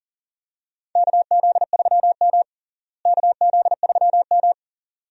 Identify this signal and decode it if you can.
Morse code, heard as a keyed tone, about 40 wpm, 700 Hz. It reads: KZ3M KZ3M
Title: USA Call Signs - 4 Characters - Repeated 2x - 40wpm